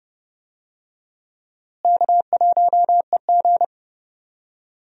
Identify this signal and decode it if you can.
Morse code, heard as a keyed tone, about 30 wpm, 700 Hz. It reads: K1EG
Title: USA Call Signs - 4 Characters - Repeated 3x - 30wpm